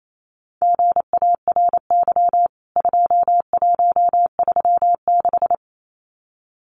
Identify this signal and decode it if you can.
Morse code, heard as a keyed tone, about 28 wpm, 700 Hz. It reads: GARY 2136
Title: CWT Exchange 28wpm